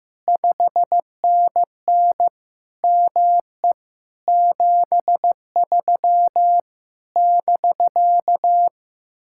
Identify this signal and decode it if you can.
Morse code, heard as a keyed tone, about 15 wpm, 700 Hz. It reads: 5NN ME 73 <BK>
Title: Speed-Racing Common QSO Phrases Encore - 15wpm